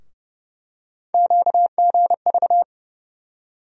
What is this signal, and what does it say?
Morse code, heard as a keyed tone, about 30 wpm, 700 Hz. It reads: QGV
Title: USA Call Sign Postfixes - 3 Letters 30wpm